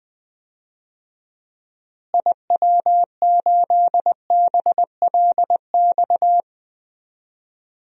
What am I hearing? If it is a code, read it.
Morse code, heard as a keyed tone, about 20 wpm, 700 Hz. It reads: IW8BLX
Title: DXCC-100 Call Signs - 6 Characters - Repeated 3x - 20wpm